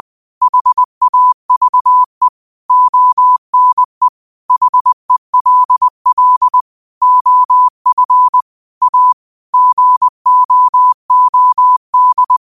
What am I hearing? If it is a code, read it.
Morse code, heard as a keyed tone, about 20 wpm, 1000 Hz. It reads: HAVE ONE HELL OF A GOOD
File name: qotdfiles / 20wpm